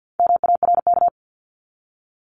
Morse code, heard as a keyed tone, about 50 wpm, 700 Hz. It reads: MARK
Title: ED Suffix - Top 100 Words with Base Words 50wpm